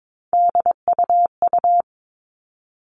Morse code, heard as a keyed tone, about 22 wpm, 700 Hz. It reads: DUU
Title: All Cut Numbers - 3-Digit 22wpm